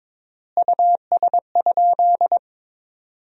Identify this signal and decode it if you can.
Morse code, heard as a keyed tone, about 21 wpm, 700 Hz. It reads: US?
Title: 3x Word-Spacing - Sentences from Top 400 Words - 22wpm